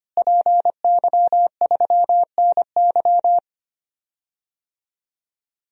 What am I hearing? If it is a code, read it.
Morse code, heard as a keyed tone, about 25 wpm, 700 Hz. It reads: PY3NY
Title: DXCC-100 Call Signs - 5 Characters - Repeated 2x - 25wpm